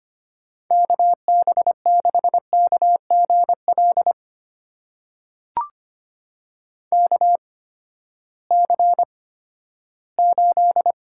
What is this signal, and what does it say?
Morse code, heard as a keyed tone, about 25 wpm, spaced as for 14 wpm, 700 Hz. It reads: KB6KGL KC8
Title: USA Call Signs - 6 Characters - 8x Character Spacing - 25wpm